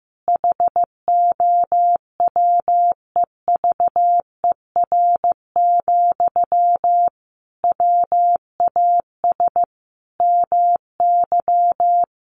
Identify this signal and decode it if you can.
Morse code, heard as a keyed tone, about 15 wpm, 700 Hz. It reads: HOWEVER, WAS MY
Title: Sentences from Top 1500 Words 15wpm